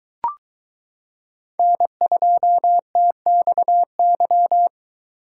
Morse code, heard as a keyed tone, about 23 wpm, 700 Hz. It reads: N2TXY